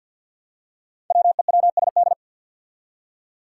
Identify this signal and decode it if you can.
Morse code, heard as a keyed tone, about 46 wpm, 700 Hz. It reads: WEWSD
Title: Five Challenging Letter-Number Combinations 50wpm